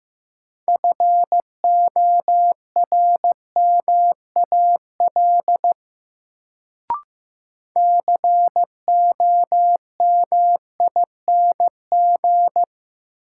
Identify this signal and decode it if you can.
Morse code, heard as a keyed tone, about 15 wpm, 700 Hz. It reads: FORMAL COMING
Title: Sentences from Top 1300 Words - New Word Drill Alt - 15wpm